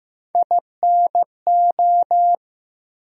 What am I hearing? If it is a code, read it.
Morse code, heard as a keyed tone, about 16 wpm, 700 Hz. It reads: INO
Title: USA Call Sign Postfixes - 3 Letters 15wpm